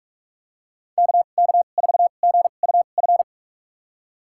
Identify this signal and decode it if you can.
Morse code, heard as a keyed tone, about 46 wpm, spaced as for 32 wpm, 700 Hz. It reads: KK4GUF
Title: USA Call Signs - 6 Characters - 2x Character Spacing - 45wpm